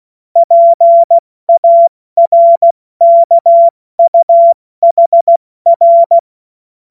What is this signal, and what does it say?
Morse code, heard as a keyed tone, about 16 wpm, 680 Hz. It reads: PARKUHR
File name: Morsekurs / Witze / Witz 02 April 2024 70 F80